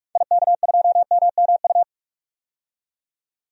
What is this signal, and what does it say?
Morse code, heard as a keyed tone, about 46 wpm, 700 Hz. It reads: IK2MMU
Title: DXCC-100 Call Signs - 6 Characters - 45wpm